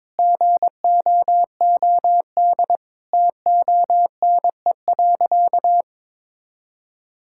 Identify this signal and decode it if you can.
Morse code, heard as a keyed tone, about 21 wpm, 700 Hz. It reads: GOOD TONE.
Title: Sentences from Top 1600 Words 22wpm